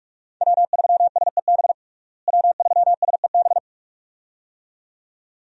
Morse code, heard as a keyed tone, about 46 wpm, 700 Hz. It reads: W3SEB W3SEB
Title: USA Call Signs - 5 Characters - Repeated 2x - 45wpm